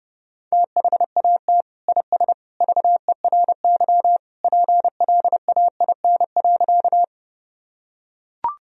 Morse code, heard as a keyed tone, about 30 wpm, 700 Hz. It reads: THAT IS VERY PLAIN.
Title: Sentences from Top 1000 Words - Review Part 2 - 30wpm